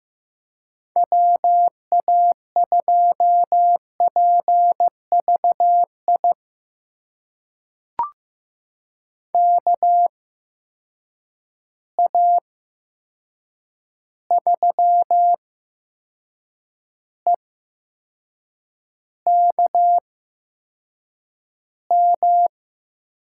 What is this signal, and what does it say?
Morse code, heard as a keyed tone, about 15 wpm, spaced as for 6 wpm, 700 Hz. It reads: WA2PVI KA3EKM
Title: USA Call Signs - 6 Characters - 8x Character Spacing - 15wpm